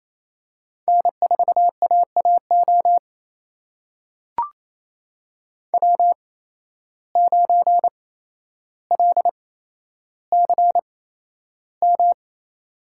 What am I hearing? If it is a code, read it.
Morse code, heard as a keyed tone, about 28 wpm, spaced as for 12 wpm, 700 Hz. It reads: N4AAO W9LCM
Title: USA Call Signs - 5 Characters - 8x Character Spacing - 28wpm